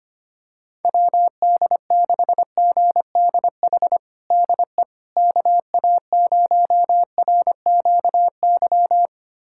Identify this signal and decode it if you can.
Morse code, heard as a keyed tone, about 25 wpm, 700 Hz. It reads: WD6GDH DE KA0RQY
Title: USA Callsign DE Callsign - 6 Characters - 25wpm